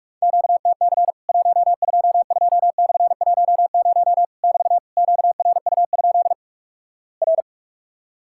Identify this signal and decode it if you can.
Morse code, heard as a keyed tone, about 46 wpm, 700 Hz. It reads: QTC 122/10 <BT> QRU? R
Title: Worked All Europe DX Contest - QTC Exchange - 45wpm